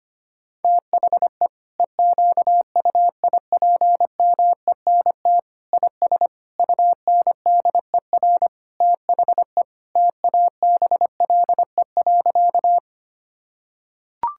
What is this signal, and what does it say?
Morse code, heard as a keyed tone, about 25 wpm, 700 Hz. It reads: THE EQUIPMENT IS UNDER THE TABLE.
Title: Sentences from Top 1000 Words - Review Part 5 - 25wpm